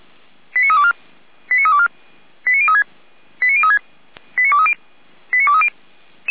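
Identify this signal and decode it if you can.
Five-tone selective call (ZVEI), heard as quick runs of five tones, tones of about 70 ms. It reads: 89324 89324 89036 89036 89230 89230
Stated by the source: ZVEI five-tone call 89329 received by SDR, recorded by feuerwehr-orlamuende.org fire brigade server.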